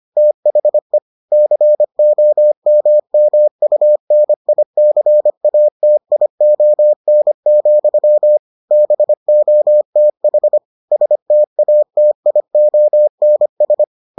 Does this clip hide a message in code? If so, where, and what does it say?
Yes: Morse code from 0.2 to 13.8 s, heard as a keyed tone, about 25 wpm, 600 Hz. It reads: THE COMMUNICATION, BOTH STATIONS